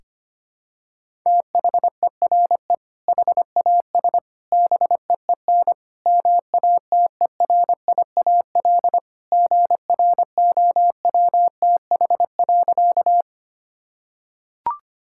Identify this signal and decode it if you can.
Morse code, heard as a keyed tone, about 25 wpm, 700 Hz. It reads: THERE HAS BEEN MATERIAL GROWTH.
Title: Sentences from Top 1000 Words - Review Part 5 - 25wpm